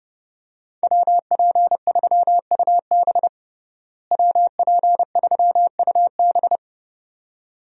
Morse code, heard as a keyed tone, about 30 wpm, 700 Hz. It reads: WP3UB WP3UB
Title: USA Call Signs - 5 Characters - Repeated 2x - 30wpm